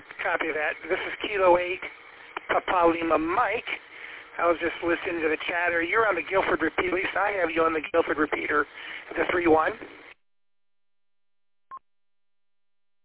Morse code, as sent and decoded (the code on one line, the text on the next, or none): none